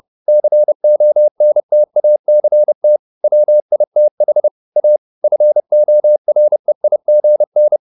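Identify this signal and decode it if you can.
Morse code, heard as a keyed tone, about 30 wpm, 600 Hz. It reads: CONTACT WITH A FOREIGN
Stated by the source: Morse code at 30 wpm, the novel Never by Ken Follett in English, from CW med Gnister.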